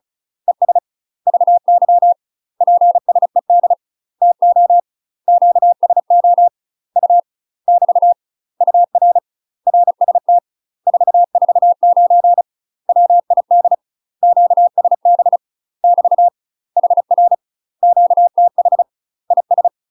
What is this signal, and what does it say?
Morse code, heard as a keyed tone, about 34 wpm, 700 Hz. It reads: ES VY PSED TO QSO U <BT> UR RST 449 WID QSB <BT> HR QTH IS
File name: UK MASTER QSOs TRAINING/35WPM